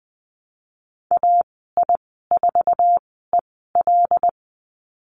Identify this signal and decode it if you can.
Morse code, heard as a keyed tone, about 20 wpm, spaced as for 15 wpm, 700 Hz. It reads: AI4EL
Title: 088 Lesson 15g - Number 5 - Call Signs 20wpm